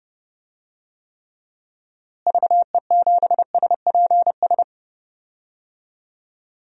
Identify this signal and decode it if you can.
Morse code, heard as a keyed tone, about 30 wpm, 700 Hz. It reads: VE7SPS
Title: DXCC-100 Call Signs - 6 Characters - Repeated 2x - 30wpm